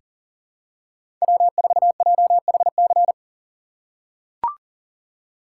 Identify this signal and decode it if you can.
Morse code, heard as a keyed tone, about 40 wpm, 700 Hz. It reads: W4JHC
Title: USA Call Signs - 6x Character Spacing - 40wpm